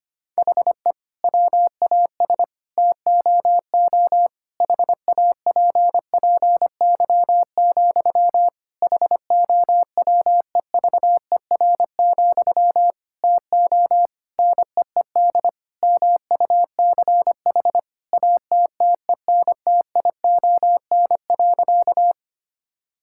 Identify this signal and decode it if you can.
Morse code, heard as a keyed tone, about 25 wpm, 700 Hz. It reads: HE WAS TOO HAPPY, HOWEVER, TO NEED MUCH ATTENTION.
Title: Sentences from Top 1000 Words - Review Part 6 - 25wpm